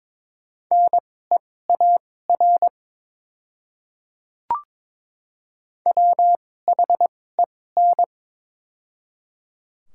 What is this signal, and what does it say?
Morse code, heard as a keyed tone, about 22 wpm, spaced as for 16 wpm, 690 Hz. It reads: NEAR WHEN
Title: 2x Character Spacing - Top 200 Words - 22wpm